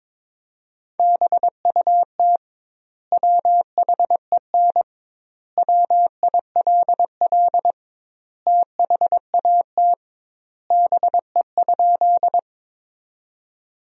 Morse code, heard as a keyed tone, about 22 wpm, 700 Hz. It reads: BUT WHEN WILL THAT BE?
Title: 2x Word-Spacing - Sentences from Top 100 Words - 22wpm